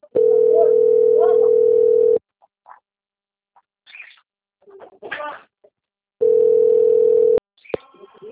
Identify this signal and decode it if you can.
Ringback tone (North America), 440 + 480 Hz.